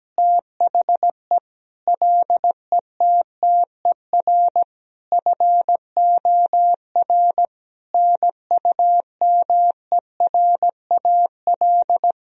Morse code, heard as a keyed tone, about 17 wpm, 700 Hz. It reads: THE LETTER FOR NUMERAL
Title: Sentences from Top 1000 Words - Review Part 2 - 17wpm